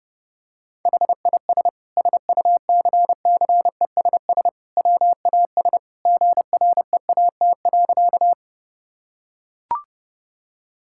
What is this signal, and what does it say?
Morse code, heard as a keyed tone, about 32 wpm, 700 Hz. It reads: HIS SUCCESS WAS GREAT.
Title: Sentences from Top 1000 Words - Review Part 6 - 30wpm